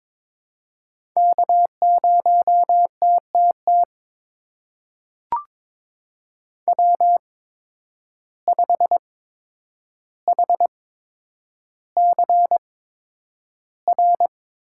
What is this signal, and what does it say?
Morse code, heard as a keyed tone, about 22 wpm, spaced as for 9 wpm, 700 Hz. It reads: K0TTT W5HCR
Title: USA Call Signs - 5 Characters - 8x Character Spacing - 22wpm